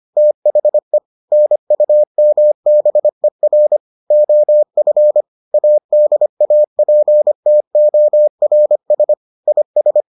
Morse code, heard as a keyed tone, about 25 wpm, 600 Hz. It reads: THE NUMBER OF ADAPTORS IS